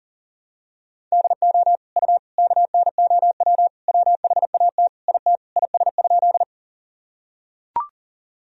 Morse code, heard as a keyed tone, about 40 wpm, 700 Hz. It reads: DO U KNOW WHAT IT IS?
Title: Sentences from Top 100 Words and Abbreviations 40wpm